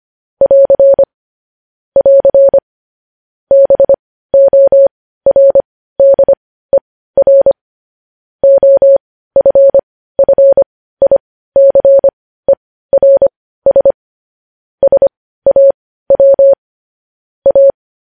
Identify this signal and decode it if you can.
Morse code, heard as a keyed tone, about 25 wpm, spaced as for 15 wpm, 550 Hz. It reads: <AR> <AR> BORDER OFFICERS SAW A
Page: News headlines at 15